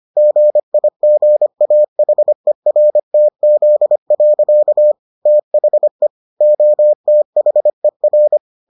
Morse code, heard as a keyed tone, about 25 wpm, 600 Hz. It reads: GIGAHERTZ. THE OTHER